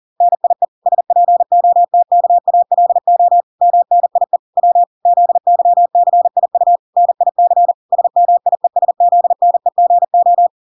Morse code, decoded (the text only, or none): NIE SPOTKALO MNIE W ZYCIU NIC SMIESZNEGO